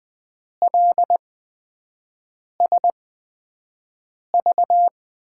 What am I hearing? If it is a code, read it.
Morse code, heard as a keyed tone, about 20 wpm, spaced as for 5 wpm, 700 Hz. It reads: LSV
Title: USA Call Signs - 8x Character Spacing - 20wpm